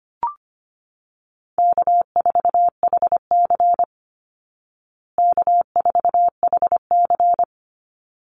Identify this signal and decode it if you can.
Morse code, heard as a keyed tone, about 25 wpm, 700 Hz. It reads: K4HC K4HC
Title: Easy Call Signs Repeated 2x - 25wpm